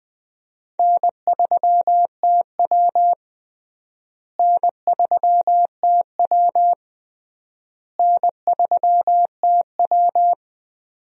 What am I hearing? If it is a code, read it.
Morse code, heard as a keyed tone, about 20 wpm, 700 Hz. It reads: N3TW N3TW N3TW
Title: USA Call Signs - 4 Characters - Repeated 3x - 20wpm